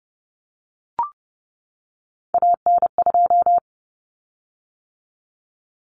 Morse code, heard as a keyed tone, about 30 wpm, 700 Hz. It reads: AN2
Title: Common Cut Numbers - 3-Digit 30wpm